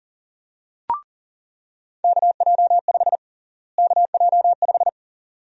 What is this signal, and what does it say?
Morse code, heard as a keyed tone, about 41 wpm, 700 Hz. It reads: KJ5 KJ5
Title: USA Call Sign Prefixes - Repeated 2x - 40wpm